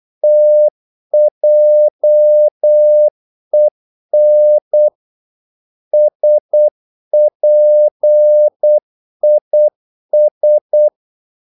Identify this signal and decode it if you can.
Morse code, heard as a keyed tone, about 8 wpm, 600 Hz. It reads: TJEN SPIS
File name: Korte ord DK 8wpm